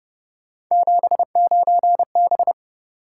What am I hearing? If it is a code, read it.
Morse code, heard as a keyed tone, about 30 wpm, 700 Hz. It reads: 79B